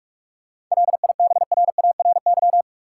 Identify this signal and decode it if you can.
Morse code, heard as a keyed tone, about 46 wpm, 700 Hz. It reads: LIBRARY
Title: Seven Letter Words - 45wpm